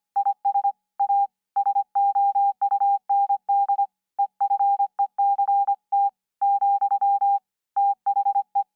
Morse code, heard as a keyed tone, about 25 wpm, 800 Hz. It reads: IS A SOUND EFECT , THE